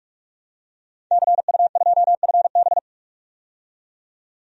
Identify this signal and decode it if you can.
Morse code, heard as a keyed tone, about 44 wpm, 700 Hz. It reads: CU2FB
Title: DXCC-100 Call Signs - 5 Characters - 45wpm